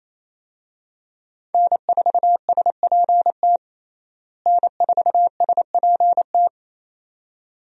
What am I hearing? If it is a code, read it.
Morse code, heard as a keyed tone, about 27 wpm, 700 Hz. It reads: N4SPT N4SPT
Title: USA Call Signs - 5 Characters - Repeated 2x - 28wpm